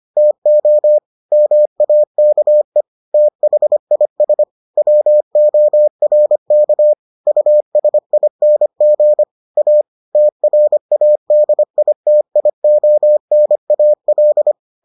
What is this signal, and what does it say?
Morse code, heard as a keyed tone, about 25 wpm, 600 Hz. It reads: TO MAKE THIS WORK USING A TRADITIONAL